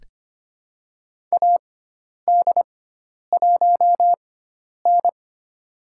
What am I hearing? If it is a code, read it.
Morse code, heard as a keyed tone, about 25 wpm, spaced as for 10 wpm, 700 Hz. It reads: AD1N
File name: course / 25:10wpm / 095 Lesson 16g - Letter c - Call Signs 25wpm